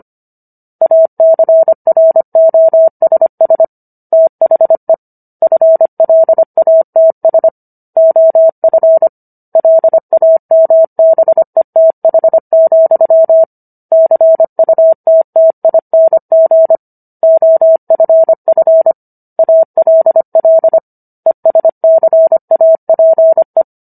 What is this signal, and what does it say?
Morse code, heard as a keyed tone, about 25 wpm, 650 Hz. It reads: ACROSS THE FLATS OF LAMBETH, CUTTING OFF ALL ESCAPE